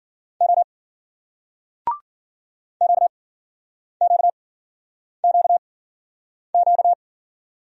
Morse code, heard as a keyed tone, about 51 wpm, spaced as for 9 wpm, 700 Hz. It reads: O QQQQ